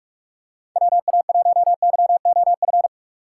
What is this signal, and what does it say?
Morse code, heard as a keyed tone, about 41 wpm, 700 Hz. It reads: WA1YOF
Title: USA Call Signs - 6 Characters - 4x Character Spacing - 45wpm